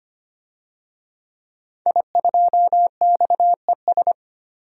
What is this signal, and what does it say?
Morse code, heard as a keyed tone, about 25 wpm, 700 Hz. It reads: I2XES
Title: DXCC-100 Call Signs - 5 Characters - 25wpm